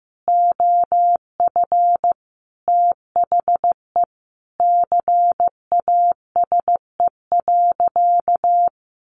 Morse code, heard as a keyed tone, about 15 wpm, 700 Hz. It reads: OF THE CASE.
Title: Sentences from Top 1400 Words 15wpm